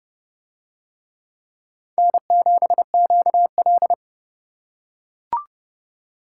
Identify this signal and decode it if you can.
Morse code, heard as a keyed tone, about 29 wpm, 700 Hz. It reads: N7QL